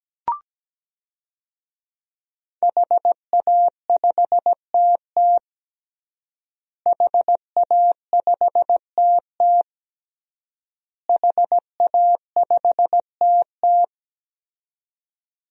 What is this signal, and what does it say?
Morse code, heard as a keyed tone, about 17 wpm, 700 Hz. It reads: HA5TT HA5TT HA5TT
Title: DXCC-100 Call Signs - 5 Characters - Repeated 3x - 17wpm